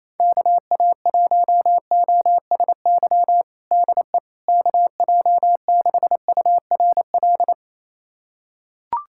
Morse code, heard as a keyed tone, about 28 wpm, 700 Hz. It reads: KA1OSY DE KJ6URL
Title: USA Callsign DE Callsign - 6 Characters - 28wpm